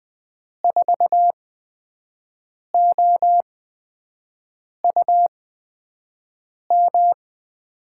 Morse code, heard as a keyed tone, about 20 wpm, spaced as for 5 wpm, 700 Hz. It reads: 4OUM
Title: USA Call Signs - 8x Character Spacing - 20wpm